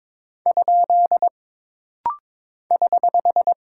Morse code, heard as a keyed tone, about 22 wpm, 700 Hz. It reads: ? <HH>
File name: practice / 22wpm / Single Punctuation-Prosign - Rapid-Fire - 22wpm